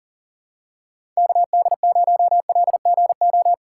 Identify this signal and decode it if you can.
Morse code, heard as a keyed tone, about 40 wpm, 700 Hz. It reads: KD0LGO
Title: USA Call Signs - 6 Characters - 8x Character Spacing - 40wpm